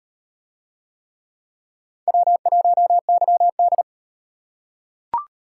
Morse code, heard as a keyed tone, about 38 wpm, 700 Hz. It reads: W1YD